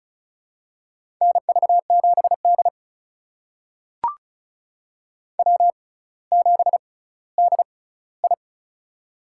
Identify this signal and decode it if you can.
Morse code, heard as a keyed tone, about 36 wpm, spaced as for 18 wpm, 700 Hz. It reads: NV7D W7DI
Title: USA Call Signs - 4 Characters - 6x Character Spacing - 35wpm